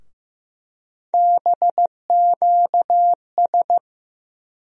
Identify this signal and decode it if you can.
Morse code, heard as a keyed tone, about 15 wpm, 700 Hz. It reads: BQS
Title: USA Call Sign Postfixes - 3 Letters 15wpm